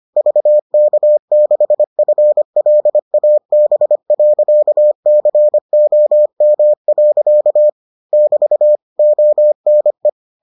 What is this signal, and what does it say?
Morse code, heard as a keyed tone, about 25 wpm, 600 Hz. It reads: VK6FLAB.COM. <BT> ONE